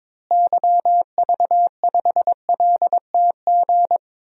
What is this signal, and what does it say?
Morse code, heard as a keyed tone, about 22 wpm, 700 Hz. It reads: YV5LTG
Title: DXCC-100 Call Signs - 6 Characters - Repeated 3x - 22wpm